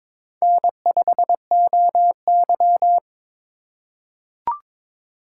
Morse code, heard as a keyed tone, about 22 wpm, 700 Hz. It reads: N5OY
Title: USA Call Signs - 4 Characters - 3x Character Spacing - 22wpm